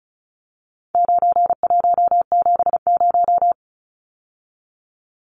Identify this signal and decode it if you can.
Morse code, heard as a keyed tone, about 34 wpm, 700 Hz. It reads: 9170
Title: Numbers - 4-Digit 35wpm